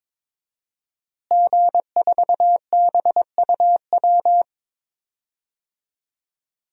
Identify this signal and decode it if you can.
Morse code, heard as a keyed tone, about 22 wpm, 700 Hz. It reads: G4BUW
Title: DXCC-100 Call Signs - 5 Characters - Repeated 3x - 22wpm